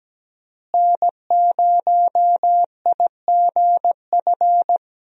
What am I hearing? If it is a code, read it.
Morse code, heard as a keyed tone, about 17 wpm, 700 Hz. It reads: N0IGF
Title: USA Call Signs - 5 Characters - Repeated 3x - 17wpm